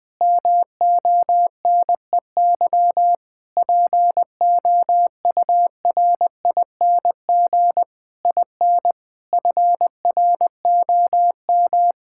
Morse code, heard as a keyed tone, about 21 wpm, 700 Hz. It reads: MONEY POURING IN FROM
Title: Sentences with 9-Word Phrases - 20wpm